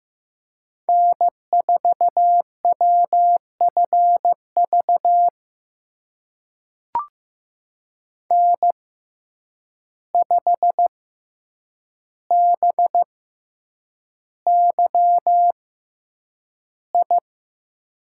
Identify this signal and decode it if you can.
Morse code, heard as a keyed tone, about 15 wpm, spaced as for 8 wpm, 700 Hz. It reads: N4WFV N5BYI
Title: USA Call Signs - 5 Characters - 6x Character Spacing - 15wpm